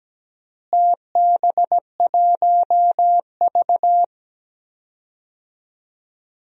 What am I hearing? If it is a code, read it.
Morse code, heard as a keyed tone, about 17 wpm, 700 Hz. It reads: TB1V
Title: DXCC-100 Call Signs - 4 Characters - Repeated 2x - 17wpm